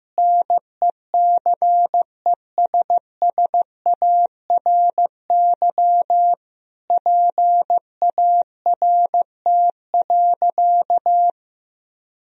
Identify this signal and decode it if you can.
Morse code, heard as a keyed tone, about 16 wpm, 700 Hz. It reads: NECESSARY PART.
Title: Sentences from Top 1000 Words - Review Part 1 - 15wpm